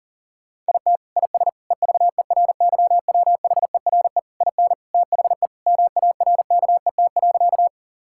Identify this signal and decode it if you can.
Morse code, heard as a keyed tone, about 40 wpm, 700 Hz. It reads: IT IS EVERYWHERE IN THE MARKET.
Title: Sentences from Top 1600 Words 40wpm